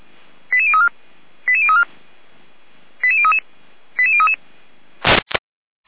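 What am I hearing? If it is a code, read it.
Five-tone selective call (ZVEI), heard as quick runs of five tones, tones of about 70 ms. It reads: 88034 88034 88033 88033
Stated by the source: Five-tone ZVEI selective call received on an SDR.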